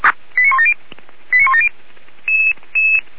Five-tone selective call (ZVEI), heard as quick runs of five tones, tones of about 70 ms. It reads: 89170 89170